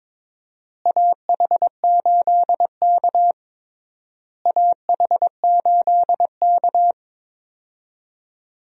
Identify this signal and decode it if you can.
Morse code, heard as a keyed tone, about 22 wpm, 700 Hz. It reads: AH8K AH8K
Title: USA Call Signs - 4 Characters - Repeated 3x - 22wpm